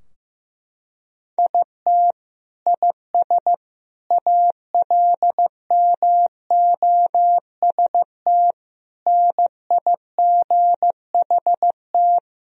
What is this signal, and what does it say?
Morse code, heard as a keyed tone, about 16 wpm, 700 Hz. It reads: IT IS ALMOST NIGHT
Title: Sentences from Top 1000 Words - Review Part 4 - 15wpm